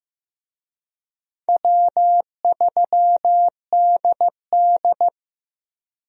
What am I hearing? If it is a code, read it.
Morse code, heard as a keyed tone, about 16 wpm, 700 Hz. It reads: W3DD
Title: USA Call Signs - 4 Characters - Repeated 2x - 15wpm